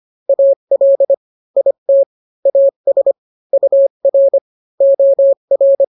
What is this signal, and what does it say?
Morse code, heard as a keyed tone, about 25 wpm, 550 Hz. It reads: AL IT AS UR OR